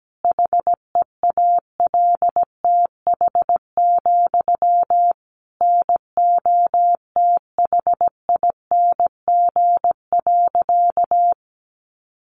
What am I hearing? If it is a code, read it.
Morse code, heard as a keyed tone, about 17 wpm, 700 Hz. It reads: HEALTH, NOTHING.
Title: Sentences from Top 900 Words 17wpm